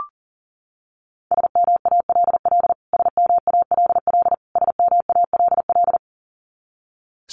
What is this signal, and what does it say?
Morse code, heard as a keyed tone, about 40 wpm, 700 Hz. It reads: SMALL SMALL SMALL
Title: Top 200 Words Repeated 40wpm